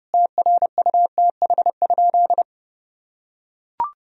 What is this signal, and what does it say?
Morse code, heard as a keyed tone, about 30 wpm, 700 Hz. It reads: TRUTH?
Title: Sentences from Top 1000 Words - Review Part 4 - 30wpm